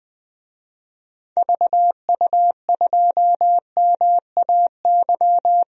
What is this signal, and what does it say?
Morse code, heard as a keyed tone, about 21 wpm, 700 Hz. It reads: VU2MAY
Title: DXCC-100 Call Signs - 6 Characters - Repeated 2x - 20wpm